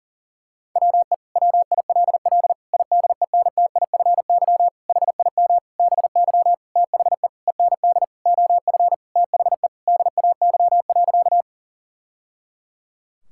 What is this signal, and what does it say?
Morse code, heard as a keyed tone, about 40 wpm, 700 Hz. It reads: WE WILL IDENTIFY HIM BY THE END OF THE DAY.
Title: Sentences from Top 1000 Words - Review Part 6 - 40wpm